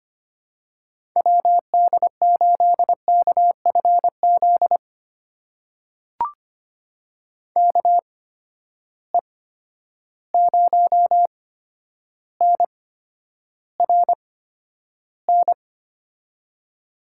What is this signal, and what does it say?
Morse code, heard as a keyed tone, about 25 wpm, spaced as for 11 wpm, 700 Hz. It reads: WD8KFZ KE0NRN
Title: USA Call Signs - 6 Characters - 8x Character Spacing - 25wpm